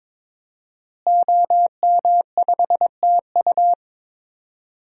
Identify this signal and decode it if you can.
Morse code, heard as a keyed tone, about 22 wpm, 700 Hz. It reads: OM5TU
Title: DXCC-100 Call Signs - 5 Characters - Repeated 3x - 22wpm